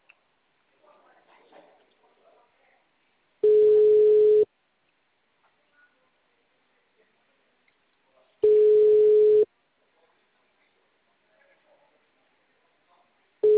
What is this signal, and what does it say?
Ringback tone (Europe, 425 Hz).
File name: out- -2015- - - .12